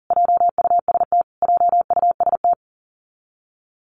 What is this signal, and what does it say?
Morse code, heard as a keyed tone, about 40 wpm, 700 Hz. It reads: JUST JUST
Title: Top 200 Words Repeated 40wpm